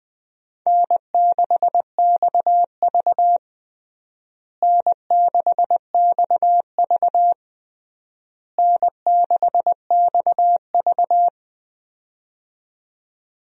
Morse code, heard as a keyed tone, about 20 wpm, 700 Hz. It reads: N6XV N6XV N6XV
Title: USA Call Signs - 4 Characters - Repeated 3x - 20wpm